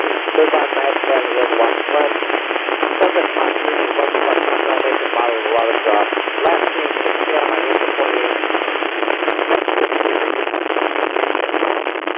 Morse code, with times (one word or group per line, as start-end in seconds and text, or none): none